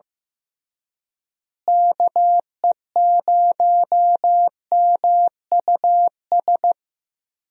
Morse code, heard as a keyed tone, about 15 wpm, 700 Hz. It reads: KE0MUS
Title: USA Call Signs - 6 Characters - Repeated 3x - 15wpm